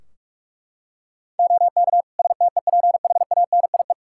Morse code, heard as a keyed tone, about 42 wpm, 700 Hz. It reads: OK STEPHANIE
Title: Practice QSO - Exchange 3 - More Info - 45wpm